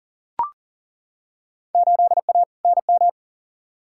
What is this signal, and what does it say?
Morse code, heard as a keyed tone, about 40 wpm, 700 Hz. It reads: 8A NM
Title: Field Day Exchanges Encore - 40wpm